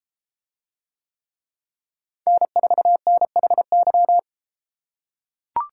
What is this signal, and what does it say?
Morse code, heard as a keyed tone, about 32 wpm, 700 Hz. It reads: N4NHY